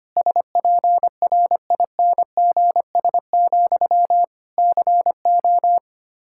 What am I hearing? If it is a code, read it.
Morse code, heard as a keyed tone, about 25 wpm, 700 Hz. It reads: SPRINGS, CO
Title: Practice QSO - Exchange 1 - Beginning - 25wpm